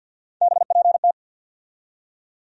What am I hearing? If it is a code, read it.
Morse code, heard as a keyed tone, about 53 wpm, 700 Hz. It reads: BPT
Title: USA Call Sign Postfixes - 3 Letters 50wpm